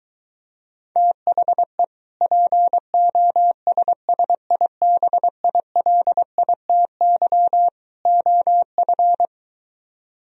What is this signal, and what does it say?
Morse code, heard as a keyed tone, about 23 wpm, 700 Hz. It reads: THE POSSIBILITY OF